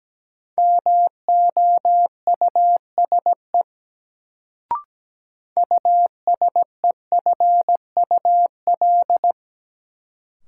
Morse code, heard as a keyed tone, about 17 wpm, 700 Hz. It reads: MOUSE USEFUL
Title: Sentences from Top 700 Words - New Word Drill - 17wpm